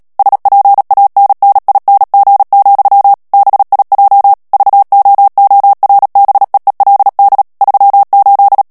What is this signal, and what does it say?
Morse code, heard as a keyed tone, about 37 wpm, 800 Hz. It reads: SPANNING, BIJ VOORBEELD 38
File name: morse mp3 / HOLLANDAIS texte clair 2 vitesse 35 mots minute